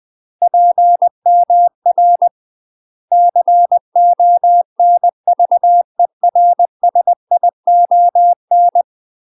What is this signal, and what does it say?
Morse code, heard as a keyed tone, about 20 wpm, 700 Hz. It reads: PMR CONVERSION
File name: UK MASTER QSOs TRAINING / 20WPM / UK MASTER QSOs 0070